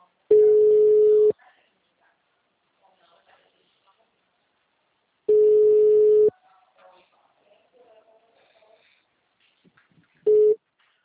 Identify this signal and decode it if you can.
Ringback tone (Europe, 425 Hz), with speech.
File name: out- -2034- - - .65616